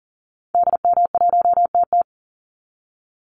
Morse code, heard as a keyed tone, about 40 wpm, 700 Hz. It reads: DM1TT
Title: Call Signs Repeated 3x - 40wpm